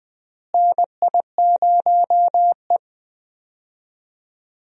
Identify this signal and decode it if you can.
Morse code, heard as a keyed tone, about 20 wpm, 700 Hz. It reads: NI0E
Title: USA Call Signs - 4 Characters - Repeated 2x - 20wpm